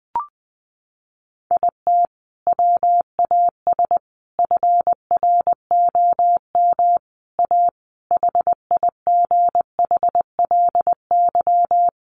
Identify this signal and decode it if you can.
Morse code, heard as a keyed tone, about 19 wpm, 700 Hz. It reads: IT WAS FROM A HIGHLY
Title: Sentences from Top 1100 Words 20wpm